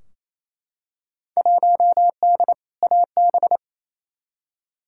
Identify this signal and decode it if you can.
Morse code, heard as a keyed tone, about 28 wpm, 700 Hz. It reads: 1D AB
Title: Field Day Exchanges Encore - 28wpm